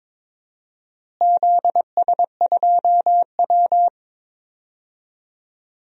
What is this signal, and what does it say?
Morse code, heard as a keyed tone, about 22 wpm, 700 Hz. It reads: ZS2W
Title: DXCC-100 Call Signs - 4 Characters - Repeated 2x - 22wpm